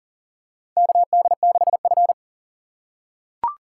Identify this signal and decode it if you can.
Morse code, heard as a keyed tone, about 37 wpm, 700 Hz. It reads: KD6F
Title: USA Call Signs - 3x Character Spacing - 40wpm